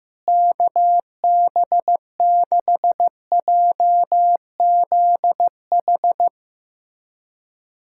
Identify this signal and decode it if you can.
Morse code, heard as a keyed tone, about 15 wpm, 700 Hz. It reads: KB6JZH
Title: USA Callsign DE Callsign - 6 Characters - 15wpm